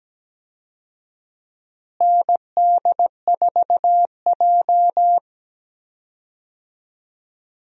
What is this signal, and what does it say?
Morse code, heard as a keyed tone, about 17 wpm, 700 Hz. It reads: ND4J
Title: USA Call Signs - 4 Characters - Repeated 3x - 17wpm